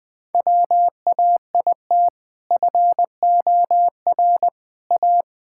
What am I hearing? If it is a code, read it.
Morse code, heard as a keyed tone, about 20 wpm, 700 Hz. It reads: WAIT FOR A
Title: Sentences with 2-Word Phrases - 20wpm